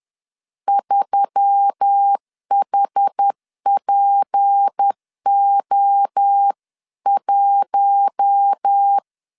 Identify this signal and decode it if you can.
Morse code, heard as a keyed tone, about 10 wpm, 800 Hz. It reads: 3HPO1
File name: LEZIONI / 8 YJ45 / 8 YJ 10wpm 2651